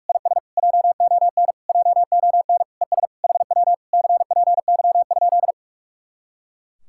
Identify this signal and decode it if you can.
Morse code, heard as a keyed tone, about 51 wpm, 700 Hz. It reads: IS JON JON ES HW CPY?
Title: Speed-Racing Common QSO Phrases Encore - 45wpm